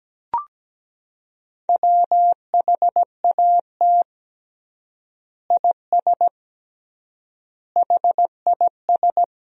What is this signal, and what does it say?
Morse code, heard as a keyed tone, about 17 wpm, 700 Hz. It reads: WHAT IS HIS
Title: 3x Word-Spacing - Sentences from Top 200 Words - 17wpm